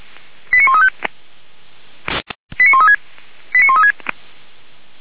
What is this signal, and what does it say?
Five-tone selective call (ZVEI), heard as quick runs of five tones, tones of about 70 ms. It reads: 89135 89136 89136
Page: ZVEI